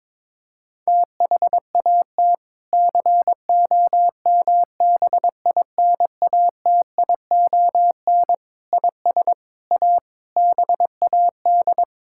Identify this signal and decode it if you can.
Morse code, heard as a keyed tone, about 22 wpm, 700 Hz. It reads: THAT COMBINATION IS A BAD
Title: Sentences from Top 1000 Words - Review Part 6 - 22wpm